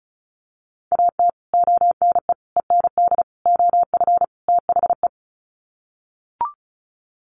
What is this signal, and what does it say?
Morse code, heard as a keyed tone, about 34 wpm, 700 Hz. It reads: AT ONE END OF THE
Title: Sets of 5 Words Encore 35wpm